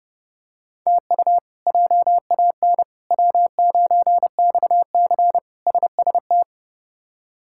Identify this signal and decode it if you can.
Morse code, heard as a keyed tone, about 31 wpm, 700 Hz. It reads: TU JAN W9XC SST
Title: K1USN SST QSO Practice - 30wpm